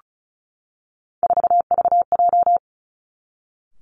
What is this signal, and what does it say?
Morse code, heard as a keyed tone, about 36 wpm, 700 Hz. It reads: 4VJ
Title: Three Challenging Letter-Number Combinations 35wpm